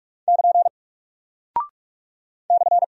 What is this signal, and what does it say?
Morse code, heard as a keyed tone, about 46 wpm, 700 Hz. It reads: <KN> /
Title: Single Character (Letter:Number:Punctuation:Prosign) Alt 45wpm